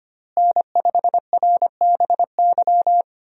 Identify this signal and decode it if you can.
Morse code, heard as a keyed tone, about 25 wpm, 700 Hz. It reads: N5RBY
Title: USA Call Signs - 5 Characters - Repeated 3x - 25wpm